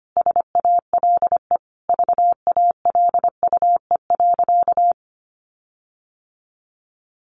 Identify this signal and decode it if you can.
Morse code, heard as a keyed tone, about 25 wpm, 700 Hz. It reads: SALE VALUE.
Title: Sentences from Top 1400 Words 25wpm